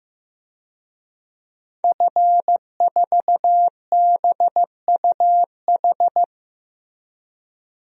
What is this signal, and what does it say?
Morse code, heard as a keyed tone, about 15 wpm, 700 Hz. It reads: F4BUH
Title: DXCC-100 Call Signs - 5 Characters - Repeated 3x - 15wpm